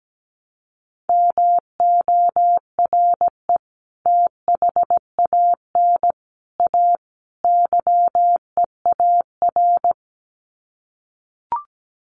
Morse code, heard as a keyed tone, about 17 wpm, 700 Hz. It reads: MORE THAN A YEAR
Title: Sets of 4 Words 17wpm